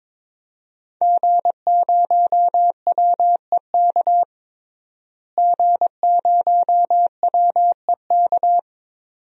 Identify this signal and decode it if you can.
Morse code, heard as a keyed tone, about 22 wpm, 700 Hz. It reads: G0WEK G0WEK
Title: DXCC-100 Call Signs - 5 Characters - Repeated 2x - 22wpm